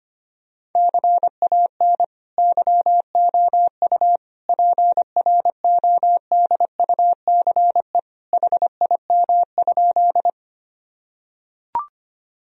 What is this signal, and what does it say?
Morse code, heard as a keyed tone, about 25 wpm, 700 Hz. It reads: CAN YOU PRODUCE HIM?
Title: Sentences from Top 1000 Words - Review Part 3 - 25wpm